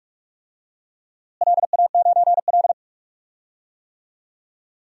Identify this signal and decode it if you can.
Morse code, heard as a keyed tone, about 46 wpm, 700 Hz. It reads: LA9L
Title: DXCC-100 Call Signs - 4 Characters - Repeated 3x - 45wpm